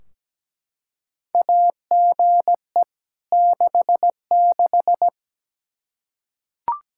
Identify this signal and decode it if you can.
Morse code, heard as a keyed tone, about 17 wpm, 700 Hz. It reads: AGE 66
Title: Practice QSO - Exchange 3 - More Info - 17wpm